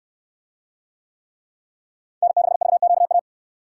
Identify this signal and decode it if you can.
Morse code, heard as a keyed tone, about 67 wpm, 700 Hz. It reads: N6VQA